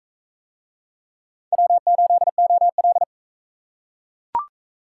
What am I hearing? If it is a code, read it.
Morse code, heard as a keyed tone, about 40 wpm, 700 Hz. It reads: W8OL